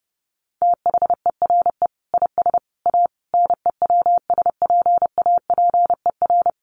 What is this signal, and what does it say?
Morse code, heard as a keyed tone, about 28 wpm, 700 Hz. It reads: THERE IS A NEWSPAPER
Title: Sentences from Top 1100 Words 30wpm